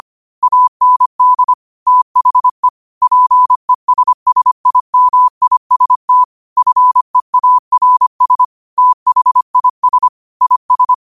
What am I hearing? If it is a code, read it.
Morse code, heard as a keyed tone, about 25 wpm, 1000 Hz. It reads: AND THE PESSIMIST FEARS THIS IS
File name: qotdfiles / 25wpm